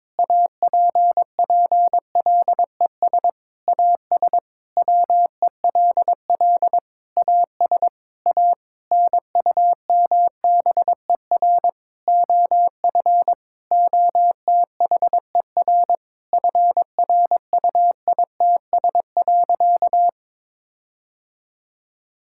Morse code, heard as a keyed tone, about 22 wpm, 700 Hz. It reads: APPLES AS WELL AS A NUMBER OF OTHER FRUITS.
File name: Sentences with 7-Word Phrases - 22wpm